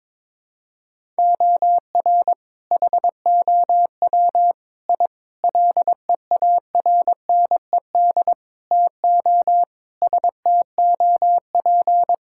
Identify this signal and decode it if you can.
Morse code, heard as a keyed tone, about 22 wpm, 700 Hz. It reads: OR HOW I LEARNED TO STOP
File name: practice / 22wpm / Sets of 8 Words - 22wpm